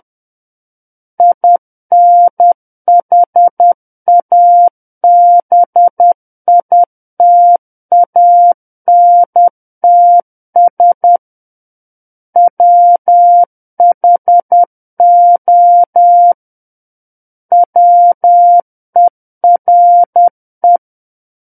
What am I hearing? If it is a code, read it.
Morse code, heard as a keyed tone, about 10 wpm, 700 Hz. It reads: INHABITANTS WHO WERE
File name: Book One / 01-Chapter 1 - 10 WPM